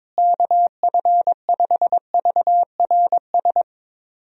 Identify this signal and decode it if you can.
Morse code, heard as a keyed tone, about 22 wpm, 700 Hz. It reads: KF5VRS
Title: USA Call Signs - 6 Characters - Repeated 2x - 22wpm